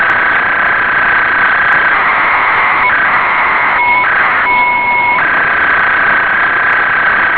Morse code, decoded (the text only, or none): none